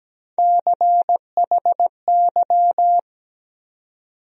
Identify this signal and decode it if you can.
Morse code, heard as a keyed tone, about 17 wpm, 700 Hz. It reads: CHY
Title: USA Call Sign Postfixes - 3 Letters 17wpm